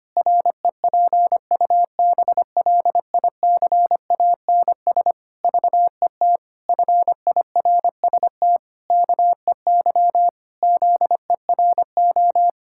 Morse code, heard as a keyed tone, about 25 wpm, 700 Hz. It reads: REPUBLICANS VET FIRST KEY ZERO
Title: Pangram Challenge v2 - 25wpm